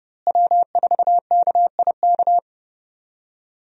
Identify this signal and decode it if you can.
Morse code, heard as a keyed tone, about 30 wpm, 700 Hz. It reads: W4KIK
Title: USA Call Signs - 5 Characters - Repeated 2x - 30wpm